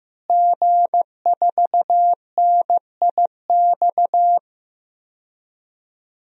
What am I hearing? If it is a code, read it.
Morse code, heard as a keyed tone, about 15 wpm, 700 Hz. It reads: G4NIX
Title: DXCC-100 Call Signs - 5 Characters - Repeated 3x - 15wpm